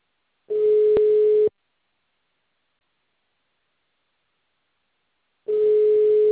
Ringback tone (Europe, 425 Hz).
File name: out- -2030- - - .95073